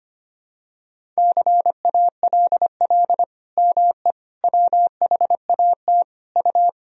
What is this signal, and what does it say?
Morse code, heard as a keyed tone, about 25 wpm, 700 Hz. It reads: CALL ME WHAT U
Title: Sentences from Top 200 Words and Abbreviations 25wpm